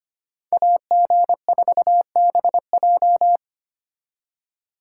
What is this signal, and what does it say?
Morse code, heard as a keyed tone, about 33 wpm, 700 Hz. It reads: AG4BJ